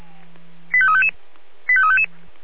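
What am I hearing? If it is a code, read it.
Five-tone selective call (ZVEI), heard as quick runs of five tones, tones of about 70 ms. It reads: 85355 85355
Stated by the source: ZVEI five-tone call 85349 received on an SDR.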